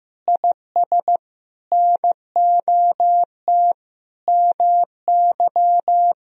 Morse code, heard as a keyed tone, about 15 wpm, 700 Hz. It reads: IS NOT MY
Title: Sentences from Top 1000 Words - Review Part 4 - 15wpm